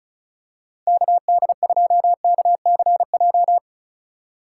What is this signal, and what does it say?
Morse code, heard as a keyed tone, about 36 wpm, 700 Hz. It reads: KD2KCJ
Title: USA Call Signs - 4x Character Spacing - 35wpm